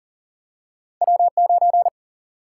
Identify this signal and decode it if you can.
Morse code, heard as a keyed tone, about 39 wpm, 700 Hz. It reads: W9
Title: Single-Letter Single-Number 40wpm